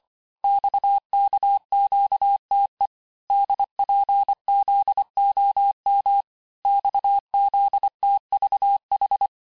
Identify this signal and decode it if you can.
Morse code, heard as a keyed tone, about 25 wpm, 800 Hz. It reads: XKQTE DPZOM XZTVH
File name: texte-chiffre-vitesse-23mpm